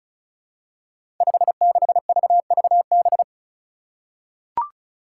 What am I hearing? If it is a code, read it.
Morse code, heard as a keyed tone, about 34 wpm, 700 Hz. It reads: 56VVB
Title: Five Challenging Letter-Number Combinations 35wpm